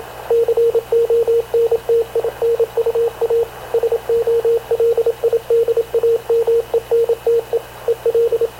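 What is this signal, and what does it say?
Morse code, heard as a keyed tone, about 28 wpm, 460 Hz, with radio noise. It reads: CONTINUA SOLIDAMENTE EL
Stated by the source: Havana Radio CLA press (PX) broadcast, recorded by Radio Officer Paul Dumesnil aboard mv CNAV QUEST/CZDO.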